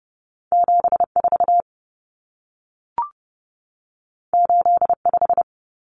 Morse code, heard as a keyed tone, about 30 wpm, 700 Hz. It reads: 74 85
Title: Numbers - 2-Digit 30wpm